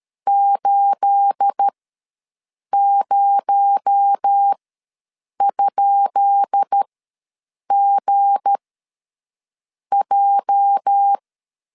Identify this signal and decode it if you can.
Morse code, heard as a keyed tone, about 13 wpm, spaced as for 6 wpm, 800 Hz. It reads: 80?GJ
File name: LEZIONI / 10 SIGNS / 10B INT ws 12wpm 3527